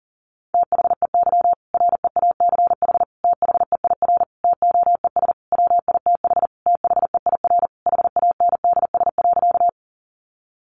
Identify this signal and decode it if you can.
Morse code, heard as a keyed tone, about 40 wpm, 700 Hz. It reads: THEY REACH THEIR TOES WITH THEIR HANDS.
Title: Easy Phrases 40wpm